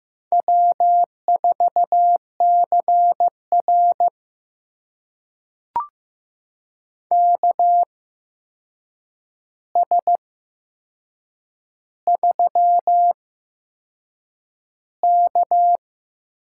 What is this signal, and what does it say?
Morse code, heard as a keyed tone, about 15 wpm, spaced as for 6 wpm, 700 Hz. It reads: W4CR KS3K
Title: USA Call Signs - 4 Characters - 8x Character Spacing - 15wpm